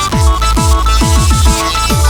There is no code.